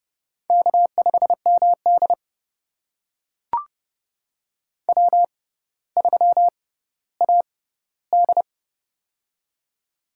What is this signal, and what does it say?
Morse code, heard as a keyed tone, about 30 wpm, spaced as for 15 wpm, 700 Hz. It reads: K5MD W3AD
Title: USA Call Signs - 4 Characters - 6x Character Spacing - 30wpm